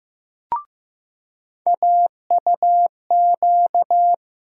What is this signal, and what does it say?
Morse code, heard as a keyed tone, about 15 wpm, 700 Hz. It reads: AUQ
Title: USA Call Sign Postfixes - 3 Letters 15wpm